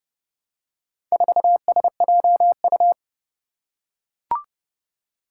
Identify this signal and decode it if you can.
Morse code, heard as a keyed tone, about 30 wpm, 700 Hz. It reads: 4SJU
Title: Four Challenging Letter-Number Combinations 30wpm